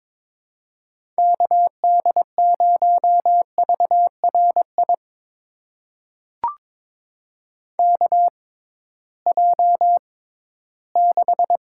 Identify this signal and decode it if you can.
Morse code, heard as a keyed tone, about 22 wpm, spaced as for 14 wpm, 700 Hz. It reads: KD0VRI KJ6
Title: USA Call Signs - 6 Characters - 6x Character Spacing - 22wpm